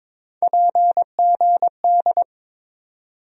Morse code, heard as a keyed tone, about 21 wpm, 700 Hz. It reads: PGD
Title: USA Call Sign Postfixes - 3 Letters 22wpm